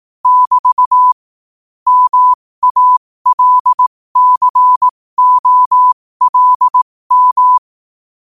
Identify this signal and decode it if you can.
Morse code, heard as a keyed tone, about 18 wpm, spaced as for 15 wpm, 1000 Hz. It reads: <BT> MALCOLM
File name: qotdfiles / 15wpm